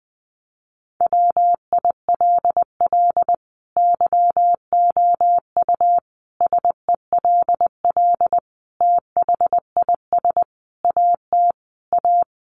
Morse code, heard as a keyed tone, about 20 wpm, 700 Hz. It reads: WILL YOU SELL THIS AT A
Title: Sentences from Top 1200 Words 20wpm